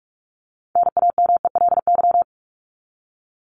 Morse code, heard as a keyed tone, about 44 wpm, 700 Hz. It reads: NAMELY
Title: LY Suffix - Top 100 Words with Base Words 45wpm